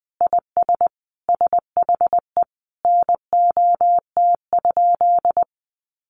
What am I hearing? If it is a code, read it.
Morse code, heard as a keyed tone, about 20 wpm, 700 Hz. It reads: IS SHE NOT?
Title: Sentences from Top 1300 Words 20wpm